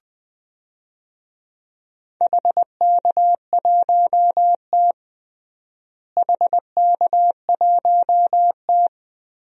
Morse code, heard as a keyed tone, about 20 wpm, 700 Hz. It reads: HK1T HK1T
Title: DXCC-100 Call Signs - 4 Characters - Repeated 3x - 20wpm